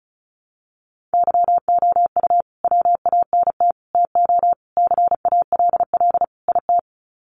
Morse code, heard as a keyed tone, about 34 wpm, 700 Hz. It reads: YOU WANT TO CALL IT
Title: Sets of 5 Words Encore2 35wpm